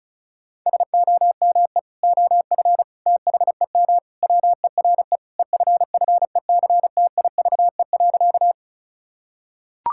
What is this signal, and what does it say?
Morse code, heard as a keyed tone, about 34 wpm, 700 Hz. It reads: SOME OF THEM WERE EFFECTIVE.
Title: Sentences from Top 1000 Words - Review Part 6 - 35wpm